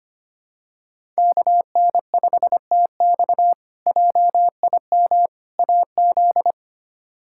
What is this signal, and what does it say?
Morse code, heard as a keyed tone, about 25 wpm, 700 Hz. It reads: KN5TX JIM AZ
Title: K1USN SST QSO Practice - 25wpm